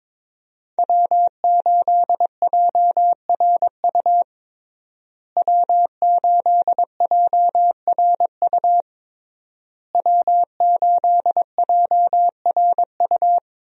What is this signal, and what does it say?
Morse code, heard as a keyed tone, about 22 wpm, 700 Hz. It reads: W8JRU W8JRU W8JRU
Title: USA Call Signs - 5 Characters - Repeated 3x - 22wpm